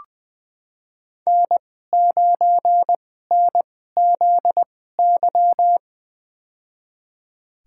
Morse code, heard as a keyed tone, about 20 wpm, spaced as for 15 wpm, 700 Hz. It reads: N9NZY
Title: USA Call Signs - 5 Characters - 2x Character Spacing - 20wpm